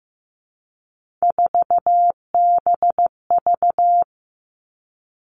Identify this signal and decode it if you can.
Morse code, heard as a keyed tone, about 15 wpm, 700 Hz. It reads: 4BV
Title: Three Challenging Letter-Number Combinations 15wpm